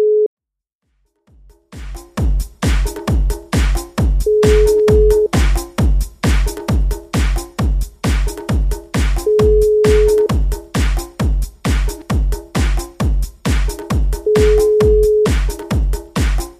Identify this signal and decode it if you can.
Ringback tone (Europe, 425 Hz).